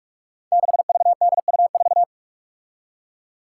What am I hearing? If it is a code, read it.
Morse code, heard as a keyed tone, about 46 wpm, 700 Hz. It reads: 6VDU4
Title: Five Challenging Letter-Number Combinations 45wpm